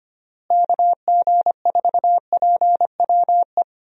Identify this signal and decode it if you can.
Morse code, heard as a keyed tone, about 25 wpm, 700 Hz. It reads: KG4PWE
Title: USA Call Signs - 6 Characters - Repeated 3x - 25wpm